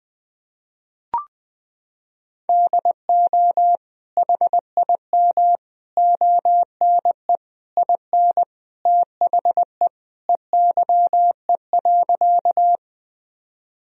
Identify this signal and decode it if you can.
Morse code, heard as a keyed tone, about 20 wpm, 700 Hz. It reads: DO HIM ONE IN THE EYE.
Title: Sentences from Top 1000 Words - Review Part 1 - 20wpm